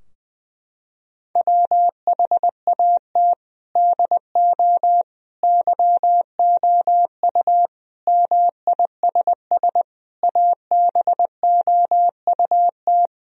Morse code, heard as a keyed tone, about 21 wpm, 700 Hz. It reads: WHAT DO YOU MISS ABOUT
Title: Sentences from Top 1000 Words - Review Part 3 - 20wpm